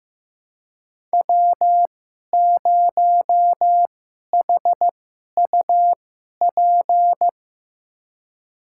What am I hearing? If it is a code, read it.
Morse code, heard as a keyed tone, about 15 wpm, spaced as for 11 wpm, 700 Hz. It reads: W0HUP
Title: USA Call Signs - 5 Characters - 2x Character Spacing - 15wpm